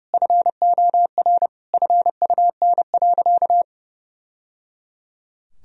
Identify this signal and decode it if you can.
Morse code, heard as a keyed tone, about 30 wpm, 700 Hz. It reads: FOR FUN.
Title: Sentences from Top 1000 Words - Review Part 3 - 30wpm